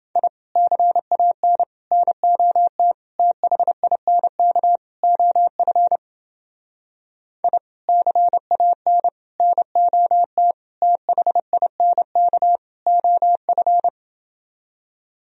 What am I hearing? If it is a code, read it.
Morse code, heard as a keyed tone, about 29 wpm, 700 Hz. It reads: I CAN NOT THINK OF I CAN NOT THINK OF